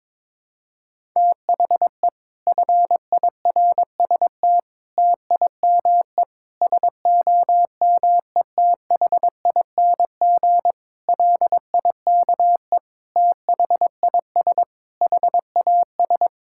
Morse code, heard as a keyed tone, about 22 wpm, 700 Hz. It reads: THE FIRST TIME SOMETHING LIKE THIS HAS
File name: practice / 22wpm / Sets of 8 Words - 22wpm